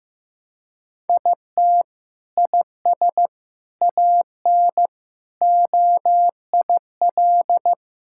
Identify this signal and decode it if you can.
Morse code, heard as a keyed tone, about 15 wpm, 700 Hz. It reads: IT IS AN OIL
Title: Sentences from Top 1000 Words - Review Part 6 - 15wpm